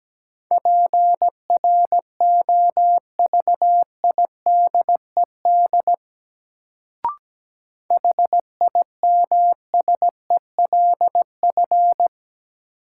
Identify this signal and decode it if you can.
Morse code, heard as a keyed tone, about 17 wpm, 700 Hz. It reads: PROVIDED HIMSELF
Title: Sentences from Top 1500 Words - New Word Drill Alt - 17wpm